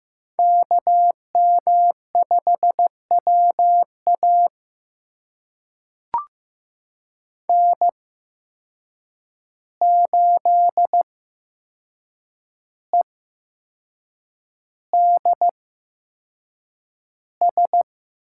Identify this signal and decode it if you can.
Morse code, heard as a keyed tone, about 15 wpm, spaced as for 6 wpm, 700 Hz. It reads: KM5WA N8EDS
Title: USA Call Signs - 5 Characters - 8x Character Spacing - 15wpm